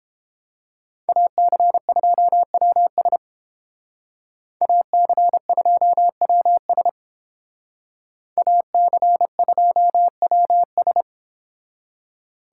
Morse code, heard as a keyed tone, about 28 wpm, 700 Hz. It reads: AC2WS AC2WS AC2WS